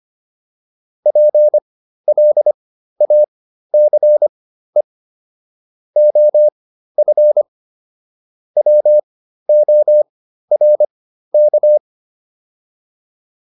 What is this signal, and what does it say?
Morse code, heard as a keyed tone, about 25 wpm, spaced as for 13 wpm, 600 Hz. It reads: PLACE OF WORK
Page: intermediate-practice-files